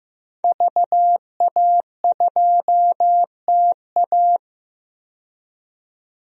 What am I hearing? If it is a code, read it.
Morse code, heard as a keyed tone, about 15 wpm, 700 Hz. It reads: VA2TA
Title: DXCC-100 Call Signs - 5 Characters - 15wpm